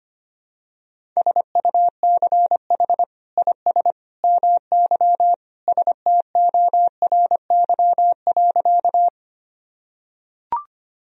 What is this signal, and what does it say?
Morse code, heard as a keyed tone, about 25 wpm, 700 Hz. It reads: SUCH IS MY STORY.
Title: Sentences from Top 1000 Words - Review Part 1 - 25wpm